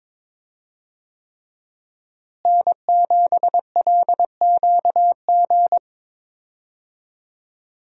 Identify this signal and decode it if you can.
Morse code, heard as a keyed tone, about 22 wpm, 700 Hz. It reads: N7LQG
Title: USA Call Signs - 5 Characters - Repeated 3x - 22wpm